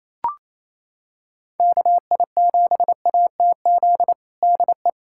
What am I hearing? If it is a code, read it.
Morse code, heard as a keyed tone, about 27 wpm, 700 Hz. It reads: KI7ATZ DE
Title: USA Callsign DE Callsign - 6 Characters - 28wpm